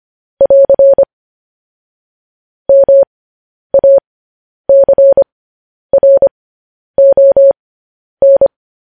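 Morse code, heard as a keyed tone, about 25 wpm, spaced as for 10 wpm, 550 Hz. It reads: <AR> MACRON
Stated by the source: Morse code news headlines broadcast, opening with vvv.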